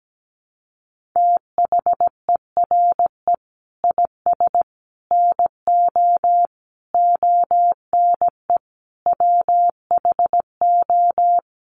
Morse code, heard as a keyed tone, about 17 wpm, 700 Hz. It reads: THERE IS NO ONE WHO
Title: Sentences from Top 1000 Words 17wpm